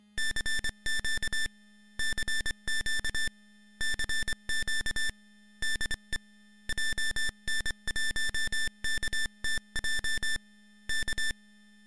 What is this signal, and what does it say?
Morse code, heard as a keyed tone, about 24 wpm, 1700 Hz, with noise. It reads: CQ CQ CQ DE JN1KTJ K
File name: GokurakuChuukaTaisen-MorseCode.ogg / GokurakuChuukaTaisen-MorseCode.ogg